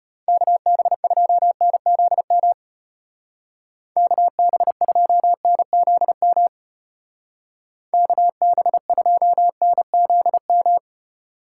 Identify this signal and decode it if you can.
Morse code, heard as a keyed tone, about 30 wpm, 700 Hz. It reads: KB2NZM KB2NZM KB2NZM
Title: Speed Racing - USA 2x3 Call Signs 25wpm